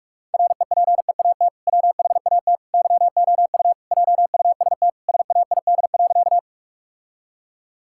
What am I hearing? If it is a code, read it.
Morse code, heard as a keyed tone, about 43 wpm, 700 Hz. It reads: REPEAT WHAT YOU JUST SAID.
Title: Sentences from Top 1600 Words 45wpm